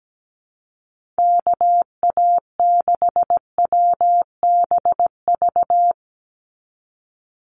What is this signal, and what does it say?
Morse code, heard as a keyed tone, about 17 wpm, 700 Hz. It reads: KA6WBV
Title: USA Call Signs 17wpm